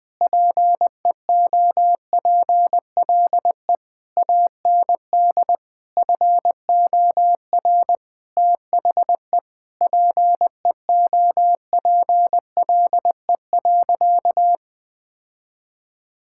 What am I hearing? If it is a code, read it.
Morse code, heard as a keyed tone, about 20 wpm, 700 Hz. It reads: PEOPLE AND FOR THE PEOPLE.
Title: Sentences with 8-Word Phrases - Spoken Beginning - 20wpm